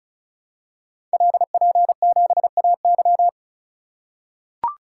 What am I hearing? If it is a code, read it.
Morse code, heard as a keyed tone, about 35 wpm, 690 Hz. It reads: LP7AY
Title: Code Groups - 5 Letter:Numbers - 35wpm